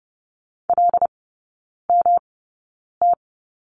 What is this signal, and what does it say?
Morse code, heard as a keyed tone, about 30 wpm, spaced as for 9 wpm, 700 Hz. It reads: LMT
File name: Lesson 18e - Letter m - Three Characters 30wpm